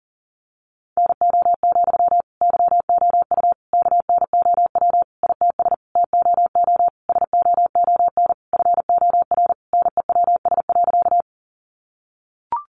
Morse code, heard as a keyed tone, about 40 wpm, 700 Hz. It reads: NO, YOU KNOW ITS TOO SOON FOR NEWS.
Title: Sentences from Top 800 Words 40wpm